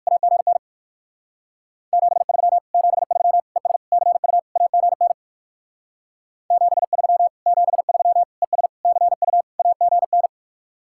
Morse code, heard as a keyed tone, about 49 wpm, 700 Hz. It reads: AGN 73 73 ES CU AGN 73 73 ES CU AGN
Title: Speed-Racing Common QSO Phrases - 45wpm